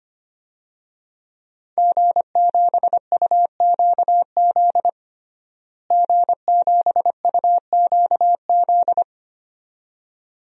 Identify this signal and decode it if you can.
Morse code, heard as a keyed tone, about 25 wpm, 700 Hz. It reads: G7UQZ G7UQZ
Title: DXCC-100 Call Signs - 5 Characters - Repeated 2x - 25wpm